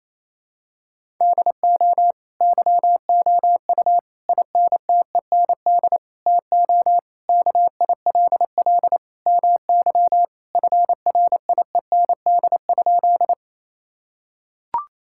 Morse code, heard as a keyed tone, about 27 wpm, 700 Hz. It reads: DO YOU INTEND TO KILL MY FRIEND?
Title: Sentences from Top 1600 Words 28wpm